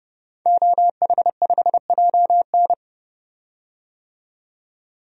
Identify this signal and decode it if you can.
Morse code, heard as a keyed tone, about 30 wpm, 700 Hz. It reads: OH5JN
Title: DXCC-100 Call Signs - 5 Characters - 30wpm